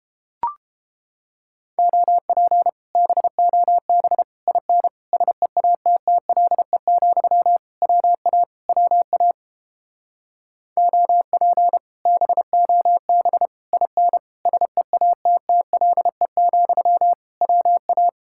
Morse code, heard as a keyed tone, about 31 wpm, 700 Hz. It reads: OP BOB IN SEATTLE, WA WA OP BOB IN SEATTLE, WA